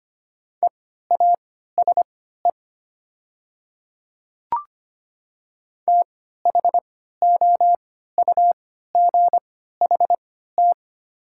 Morse code, heard as a keyed tone, about 25 wpm, spaced as for 14 wpm, 700 Hz. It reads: EASE THOUGHT
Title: 3x Character Spacing - Top 300 Words - 25wpm